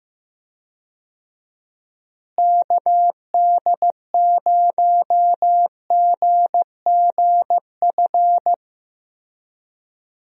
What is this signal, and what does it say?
Morse code, heard as a keyed tone, about 15 wpm, 700 Hz. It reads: KD0GGF
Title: USA Call Signs - 6 Characters - Repeated 2x - 15wpm